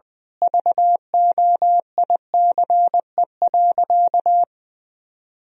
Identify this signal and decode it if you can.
Morse code, heard as a keyed tone, about 20 wpm, 700 Hz. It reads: VOICE.
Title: Sentences from Top 1000 Words - Review Part 1 - 20wpm